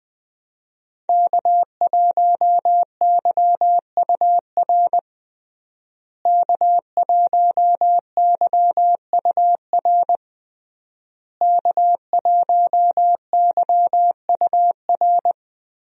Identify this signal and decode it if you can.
Morse code, heard as a keyed tone, about 20 wpm, 700 Hz. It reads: K1YUR K1YUR K1YUR
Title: USA Call Signs - 5 Characters - Repeated 3x - 20wpm